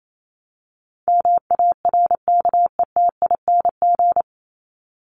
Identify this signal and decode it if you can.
Morse code, heard as a keyed tone, about 28 wpm, 700 Hz. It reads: MARKETING
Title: Top 751-1000 Words 28wpm